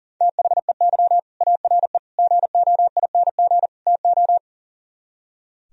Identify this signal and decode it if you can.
Morse code, heard as a keyed tone, about 52 wpm, 700 Hz. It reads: THEY ARE GOING TO